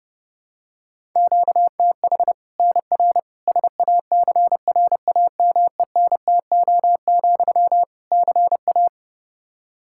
Morse code, heard as a keyed tone, about 30 wpm, 700 Hz. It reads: QTH NR SACRAMENTO, CA
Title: Practice QSO - Exchange 1 - Beginning - 30wpm